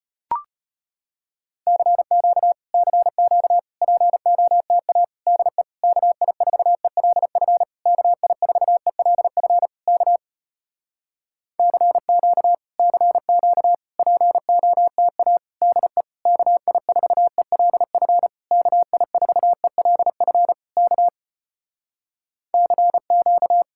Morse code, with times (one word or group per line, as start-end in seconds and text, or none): 1.7-2.5 CQ
2.7-3.6 CQ
3.8-5.0 POTA
5.3-5.6 DE
5.8-7.6 KI4ELF
7.9-9.7 KI4ELF
9.9-10.2 K
11.6-12.5 CQ
12.8-13.7 CQ
14.0-15.4 POTA
15.6-16.0 DE
16.3-18.3 KI4ELF
18.5-20.5 KI4ELF
20.8-21.1 K
22.5-23.6 CQ